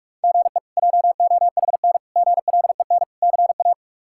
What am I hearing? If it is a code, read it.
Morse code, heard as a keyed tone, about 41 wpm, 700 Hz. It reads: GE JOHN GLEN CA
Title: K1USN SST QSO Practice - 45wpm